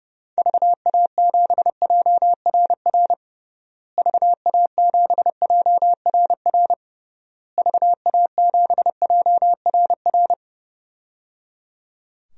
Morse code, heard as a keyed tone, about 29 wpm, 700 Hz. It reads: VA7JRR VA7JRR VA7JRR
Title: DXCC-100 Call Signs - 6 Characters - Repeated 3x - 30wpm